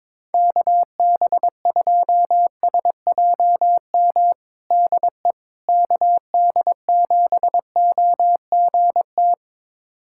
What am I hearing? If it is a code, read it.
Morse code, heard as a keyed tone, about 22 wpm, 700 Hz. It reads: KB2SJM DE KD7OGT
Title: USA Callsign DE Callsign - 6 Characters - 22wpm